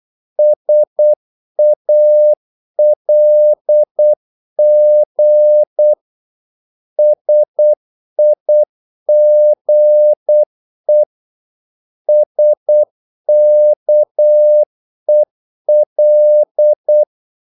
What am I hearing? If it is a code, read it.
Morse code, heard as a keyed tone, about 8 wpm, 600 Hz. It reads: SALG SIGE SKEL
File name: Korte ord DK 8wpm